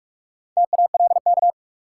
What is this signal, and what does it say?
Morse code, heard as a keyed tone, about 46 wpm, 700 Hz. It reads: TALK
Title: Sentences from Top 400 Words - New Word Drill - 45wpm